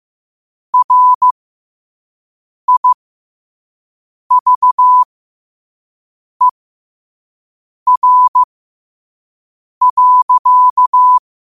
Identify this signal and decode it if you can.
Morse code, heard as a keyed tone, about 15 wpm, spaced as for 5 wpm, 1000 Hz. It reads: RIVER.
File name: qotdfiles / 5wpm